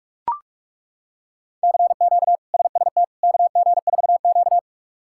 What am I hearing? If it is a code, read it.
Morse code, heard as a keyed tone, about 42 wpm, 700 Hz. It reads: CQ SST KG4Q
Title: K1USN SST QSO Practice - 45wpm